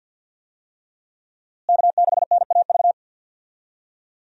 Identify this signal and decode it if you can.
Morse code, heard as a keyed tone, about 50 wpm, 700 Hz. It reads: K6NAV
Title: USA Call Signs - 5 Characters - Repeated 3x - 50wpm